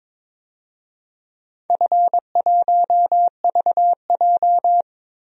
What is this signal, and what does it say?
Morse code, heard as a keyed tone, about 22 wpm, 700 Hz. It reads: F1VJ
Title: DXCC-100 Call Signs - 4 Characters - Repeated 3x - 22wpm